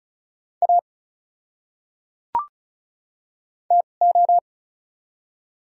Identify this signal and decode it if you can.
Morse code, heard as a keyed tone, about 36 wpm, spaced as for 25 wpm, 700 Hz. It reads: A TO
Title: 2x Character Spacing - Top 100 Words - 35wpm